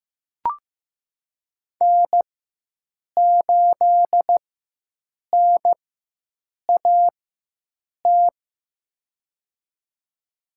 Morse code, heard as a keyed tone, about 15 wpm, spaced as for 7 wpm, 700 Hz. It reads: N8NAT
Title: USA Call Signs - 5 Characters - 4x Character Spacing - 15wpm